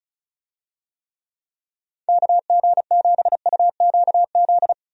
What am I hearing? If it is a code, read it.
Morse code, heard as a keyed tone, about 34 wpm, 700 Hz. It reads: KG7UQZ
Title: USA Call Signs - 6 Characters - Repeated 2x - 35wpm